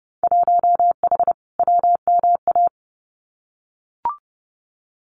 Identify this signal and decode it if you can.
Morse code, heard as a keyed tone, about 30 wpm, 700 Hz. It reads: 1H WMA
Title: Winter Field Day Exchanges 30wpm